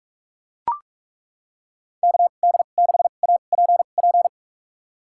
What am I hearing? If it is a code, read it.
Morse code, heard as a keyed tone, about 42 wpm, spaced as for 31 wpm, 700 Hz. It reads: KD6APP
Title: USA Call Signs - 6 Characters - 2x Character Spacing - 45wpm